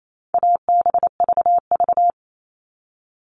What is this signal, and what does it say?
Morse code, heard as a keyed tone, about 28 wpm, 700 Hz. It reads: ABVV
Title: All Cut Numbers - 4-Digit 28wpm